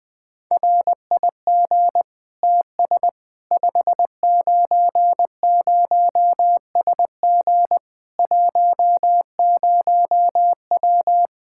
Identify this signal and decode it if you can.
Morse code, heard as a keyed tone, about 20 wpm, 700 Hz. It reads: RIG TS 590SG 10W
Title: K1USN SST QSO Practice 20wpm